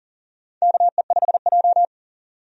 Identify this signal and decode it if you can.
Morse code, heard as a keyed tone, about 39 wpm, 700 Hz. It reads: KE5J
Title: USA Call Signs - 4 Characters - 40wpm